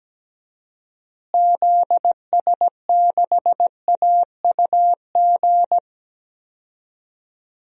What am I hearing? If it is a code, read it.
Morse code, heard as a keyed tone, about 17 wpm, 700 Hz. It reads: ZS6AUG
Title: DXCC-100 Call Signs - 6 Characters - Repeated 2x - 17wpm